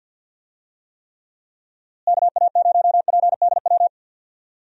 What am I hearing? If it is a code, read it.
Morse code, heard as a keyed tone, about 50 wpm, 700 Hz. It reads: KA0PDW
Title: USA Call Signs - 6 Characters - Repeated 3x - 50wpm